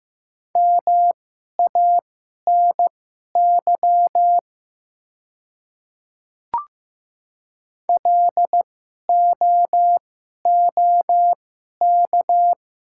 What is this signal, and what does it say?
Morse code, heard as a keyed tone, about 15 wpm, spaced as for 11 wpm, 700 Hz. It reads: MANY LOOK
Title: 2x Character Spacing - Top 100 Words - 15wpm